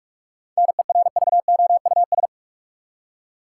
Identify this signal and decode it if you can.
Morse code, heard as a keyed tone, about 46 wpm, 700 Hz. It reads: NERVOUS
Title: Seven Letter Words - 45wpm